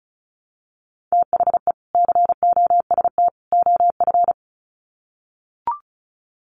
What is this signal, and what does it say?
Morse code, heard as a keyed tone, about 34 wpm, 700 Hz. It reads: THE COST OF
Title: Sets of 3 Words 35wpm